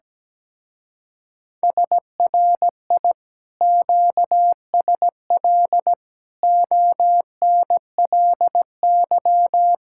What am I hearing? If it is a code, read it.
Morse code, heard as a keyed tone, about 17 wpm, 700 Hz. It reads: SRI QSL ONLY
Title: Speed-Racing Common QSO Phrases - 17wpm